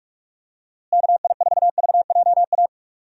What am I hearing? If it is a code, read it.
Morse code, heard as a keyed tone, about 42 wpm, 700 Hz. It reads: KI4VJA
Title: USA Call Signs - 6 Characters - Repeated 2x - 45wpm